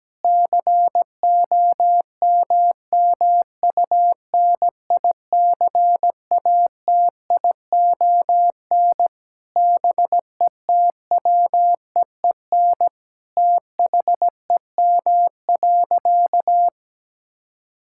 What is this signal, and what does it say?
Morse code, heard as a keyed tone, about 17 wpm, 700 Hz. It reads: COMMUNICATION BETWEEN THEM.
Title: Sentences from Top 1000 Words - Review Part 6 - 17wpm